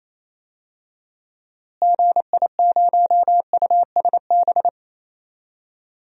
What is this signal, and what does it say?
Morse code, heard as a keyed tone, about 28 wpm, 700 Hz. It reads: GI0USB
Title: DXCC-100 Call Signs - 6 Characters - Repeated 3x - 28wpm